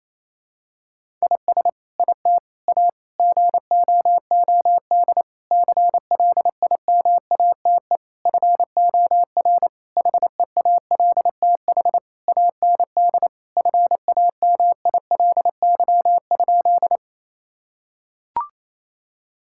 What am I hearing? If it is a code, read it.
Morse code, heard as a keyed tone, about 28 wpm, 700 Hz. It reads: IS IT A GOOD CLIMATE FOR HEALTH AND FAMILY?
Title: Sentences from Top 1600 Words 28wpm